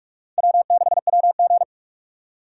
Morse code, heard as a keyed tone, about 44 wpm, 700 Hz. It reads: W6WG